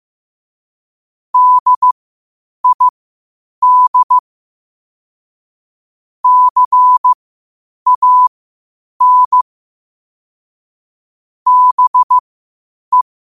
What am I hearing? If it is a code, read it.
Morse code, heard as a keyed tone, about 15 wpm, spaced as for 8 wpm, 1000 Hz. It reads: DID CAN BE
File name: qotdfiles / 8wpm